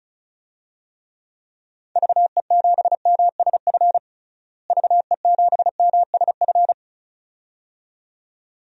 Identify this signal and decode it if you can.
Morse code, heard as a keyed tone, about 34 wpm, 700 Hz. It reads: VE7MSF VE7MSF
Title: DXCC-100 Call Signs - 6 Characters - Repeated 2x - 35wpm